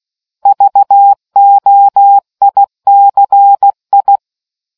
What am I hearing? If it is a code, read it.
Morse code, heard as a keyed tone, about 16 wpm, 800 Hz. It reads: VOICI